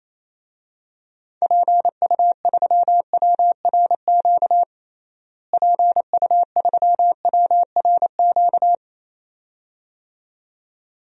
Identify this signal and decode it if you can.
Morse code, heard as a keyed tone, about 28 wpm, 700 Hz. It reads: PU3WRQ PU3WRQ
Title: DXCC-100 Call Signs - 6 Characters - Repeated 2x - 28wpm